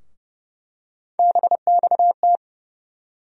Morse code, heard as a keyed tone, about 31 wpm, 700 Hz. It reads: BXT
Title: USA Call Sign Postfixes - 3 Letters 30wpm